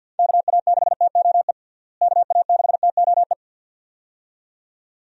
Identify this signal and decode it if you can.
Morse code, heard as a keyed tone, about 51 wpm, 700 Hz. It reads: KA6TOE KA6TOE
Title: USA Call Signs - 6 Characters - Repeated 2x - 50wpm